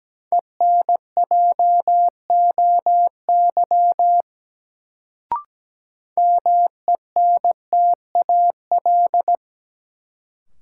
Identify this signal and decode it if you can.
Morse code, heard as a keyed tone, about 17 wpm, 700 Hz. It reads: ENJOY MENTAL
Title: Sentences from Top 1000 Words - Review Part 4 - New Word Drill - 17wpm